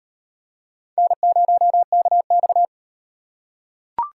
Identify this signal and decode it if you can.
Morse code, heard as a keyed tone, about 36 wpm, 700 Hz. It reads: N0KX